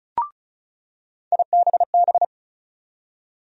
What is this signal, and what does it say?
Morse code, heard as a keyed tone, about 34 wpm, 700 Hz. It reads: IBB
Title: USA Call Sign Postfixes - 3 Letters 35wpm